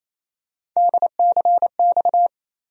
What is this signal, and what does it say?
Morse code, heard as a keyed tone, about 27 wpm, 700 Hz. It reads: DCX
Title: USA Call Sign Postfixes - 3 Letters 28wpm